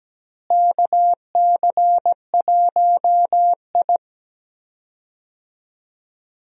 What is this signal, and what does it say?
Morse code, heard as a keyed tone, about 17 wpm, 700 Hz. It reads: KC1I
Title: USA Call Signs - 4 Characters - Repeated 3x - 17wpm